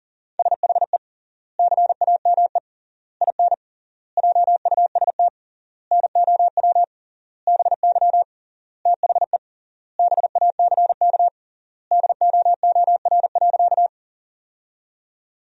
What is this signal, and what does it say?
Morse code, heard as a keyed tone, about 40 wpm, 700 Hz. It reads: SHE CAME IN JUST NOW BY THE BACK DOOR.
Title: 3x Word-Spacing - Sentences from Top 300 Words - 40wpm